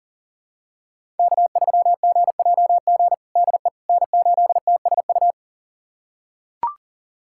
Morse code, heard as a keyed tone, about 37 wpm, 700 Hz. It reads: K3GJG DE N8TSU
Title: USA Callsign DE Callsign - 5 Characters - 40wpm